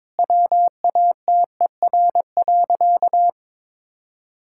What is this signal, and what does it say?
Morse code, heard as a keyed tone, about 21 wpm, 700 Hz. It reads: WATER.
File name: Difficult Sentences with Hints - 22wpm